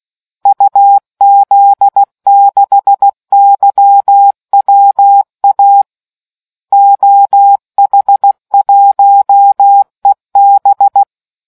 Morse code, heard as a keyed tone, about 16 wpm, 800 Hz. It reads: UZ6YWA OH1EB